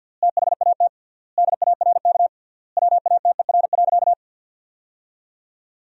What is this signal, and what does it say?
Morse code, heard as a keyed tone, about 52 wpm, 700 Hz. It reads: THAT DARK WATER.
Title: 3x Word-Spacing - Sentences from Top 500 Words - 50wpm